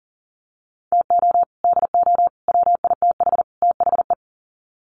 Morse code, heard as a keyed tone, about 40 wpm, 700 Hz. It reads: TO DO WITH THE
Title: Sets of 4 Words 40wpm